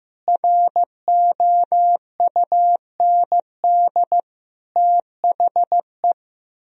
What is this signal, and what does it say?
Morse code, heard as a keyed tone, about 15 wpm, 700 Hz. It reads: ROUND THE
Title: Sentences from Top 1000 Words - Review Part 4 - 15wpm